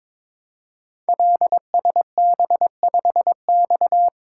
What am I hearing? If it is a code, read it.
Morse code, heard as a keyed tone, about 22 wpm, 700 Hz. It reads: LSB5X
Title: Code Groups - 5 Letter:Numbers - 22wpm